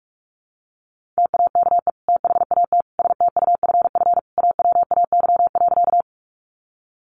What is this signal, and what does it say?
Morse code, heard as a keyed tone, about 44 wpm, 700 Hz. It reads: TAKE THAT STUFF AWAY.
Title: Sentences from Top 1400 Words 45wpm